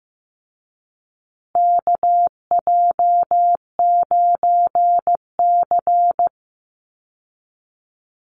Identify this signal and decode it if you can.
Morse code, heard as a keyed tone, about 15 wpm, 700 Hz. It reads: KJ9C
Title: Call Signs Repeated 3x - 15wpm